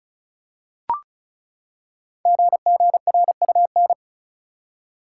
Morse code, heard as a keyed tone, about 36 wpm, 700 Hz. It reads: GGRUN
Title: Code Groups - 5 Letters - 35wpm